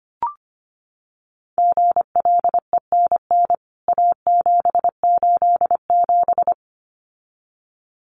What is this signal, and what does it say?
Morse code, heard as a keyed tone, about 25 wpm, 700 Hz. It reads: GLENN A787
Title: CWT Exchange 25wpm